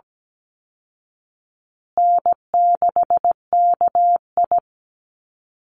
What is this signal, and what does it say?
Morse code, heard as a keyed tone, about 17 wpm, 700 Hz. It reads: N6KI
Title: Call Signs Repeated 3x - 17wpm